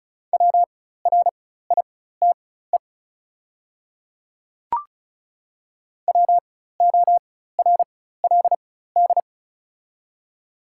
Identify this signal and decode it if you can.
Morse code, heard as a keyed tone, about 34 wpm, spaced as for 16 wpm, 690 Hz. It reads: WRITE WORLD
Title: 4x Character Spacing - Top 200 Words - 35wpm